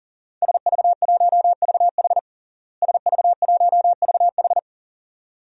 Morse code, heard as a keyed tone, about 40 wpm, 700 Hz. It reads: SV1VH SV1VH
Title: DXCC-100 Call Signs - 5 Characters - Repeated 2x - 40wpm